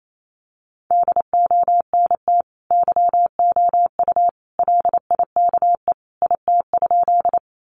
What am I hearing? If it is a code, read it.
Morse code, heard as a keyed tone, about 28 wpm, 700 Hz. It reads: DONT YOU LIKE IT?
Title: Sentences from Top 300 Words 28wpm